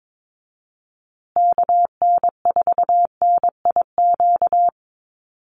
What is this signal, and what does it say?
Morse code, heard as a keyed tone, about 22 wpm, 700 Hz. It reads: KN4NIQ
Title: USA Call Signs Repeated 3x 22wpm